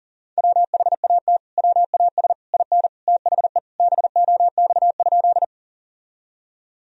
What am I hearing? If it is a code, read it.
Morse code, heard as a keyed tone, about 37 wpm, 700 Hz. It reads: WHAT WAS IN THE BOX?
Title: Sentences from Top 1000 Words - Review Part 2 - 40wpm